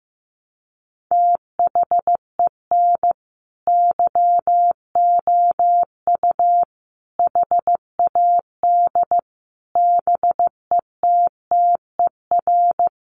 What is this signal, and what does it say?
Morse code, heard as a keyed tone, about 15 wpm, 700 Hz. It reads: THEN YOU HAD BETTER
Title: Sentences from Top 1300 Words 15wpm